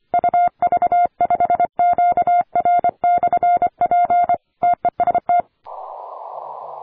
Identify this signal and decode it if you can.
Morse code, heard as a keyed tone, about 25 wpm, 700 Hz. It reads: UV5QR/P TEST
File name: attachments / 40m UU7JF P